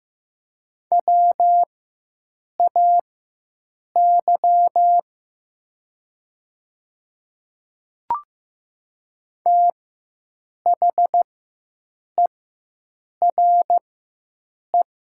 Morse code, heard as a keyed tone, about 15 wpm, spaced as for 7 wpm, 700 Hz. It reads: WAY THERE
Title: 4x Character Spacing - Top 300 Words - 15wpm